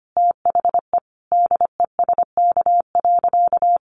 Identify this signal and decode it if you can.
Morse code, heard as a keyed tone, about 25 wpm, 700 Hz. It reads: THE DESK.
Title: A Sound Of Thunder 25wpm